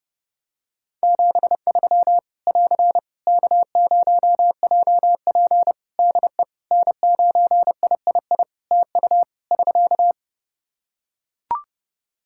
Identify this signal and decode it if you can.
Morse code, heard as a keyed tone, about 30 wpm, 700 Hz. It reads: 73 <AR> K0JP DE N9III TU <SK>
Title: Practice QSO - Exchange 4 - Ending - 30wpm